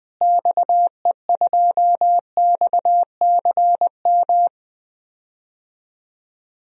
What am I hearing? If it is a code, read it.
Morse code, heard as a keyed tone, about 20 wpm, 700 Hz. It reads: XE2XCM
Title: DXCC-100 Call Signs - 6 Characters - Repeated 3x - 20wpm